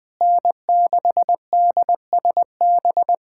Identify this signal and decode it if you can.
Morse code, heard as a keyed tone, about 20 wpm, 700 Hz. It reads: N6DSB
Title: USA Callsign DE Callsign - 5 Characters - 20wpm